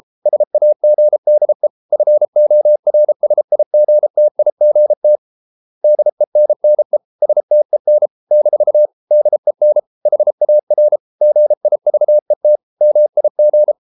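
Morse code, heard as a keyed tone, about 33 wpm, 600 Hz. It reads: SAGDE FORSIGTIGT DENNE STEN - DEN HAR GIVET MIG
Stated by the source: Morse code at 33 wpm, QRQ.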